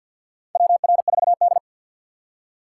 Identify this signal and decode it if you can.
Morse code, heard as a keyed tone, about 51 wpm, 700 Hz. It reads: WR4D
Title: USA Call Signs - 4 Characters - 3x Character Spacing - 50wpm